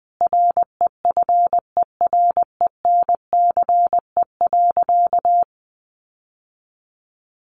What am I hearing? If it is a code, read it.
Morse code, heard as a keyed tone, about 20 wpm, 700 Hz. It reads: REFERENCE.
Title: Sentences from Top 1200 Words 20wpm